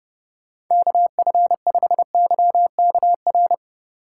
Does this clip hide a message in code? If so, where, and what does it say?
Yes: Morse code from 0.7 to 3.5 s, heard as a keyed tone, about 28 wpm, 700 Hz. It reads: KF5YKR